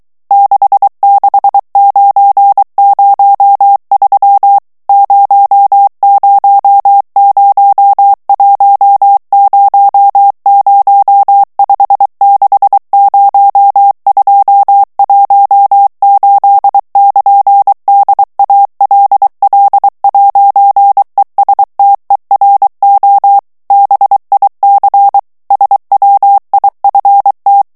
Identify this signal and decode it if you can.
Morse code, heard as a keyed tone, about 23 wpm, 800 Hz. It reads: 66903 000100560218 <KN> DALL'ESTERO BIC SWIFT
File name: ITALIEN texte clair 1 vitesse 22 mots minute